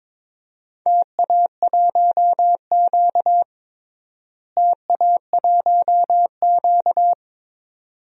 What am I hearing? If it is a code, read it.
Morse code, heard as a keyed tone, about 21 wpm, 700 Hz. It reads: TA1Q TA1Q
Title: DXCC-100 Call Signs - 4 Characters - Repeated 2x - 22wpm